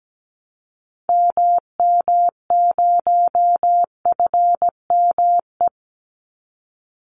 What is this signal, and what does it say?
Morse code, heard as a keyed tone, about 17 wpm, 700 Hz. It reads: MM0FME
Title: Call Signs - 17wpm